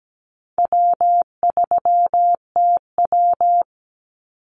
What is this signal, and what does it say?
Morse code, heard as a keyed tone, about 17 wpm, 700 Hz. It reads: W3TW
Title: Easy Call Signs Repeated 3x - 17wpm